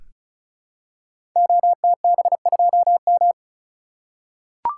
Morse code, heard as a keyed tone, about 35 wpm, 700 Hz. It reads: OTB2M
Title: Code Groups - 5 Letter:Numbers - 35wpm